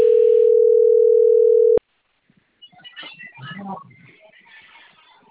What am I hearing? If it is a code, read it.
Ringback tone (North America), 440 + 480 Hz.